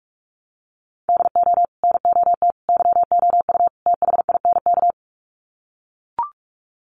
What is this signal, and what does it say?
Morse code, heard as a keyed tone, about 44 wpm, 700 Hz. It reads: DO NOT YOU THINK
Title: Sets of 4 Words 45wpm